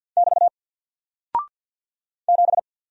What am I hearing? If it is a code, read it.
Morse code, heard as a keyed tone, about 48 wpm, 700 Hz. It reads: <BT> 7
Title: Single Character (Letter:Number:Punctuation:Prosign) Alt 50wpm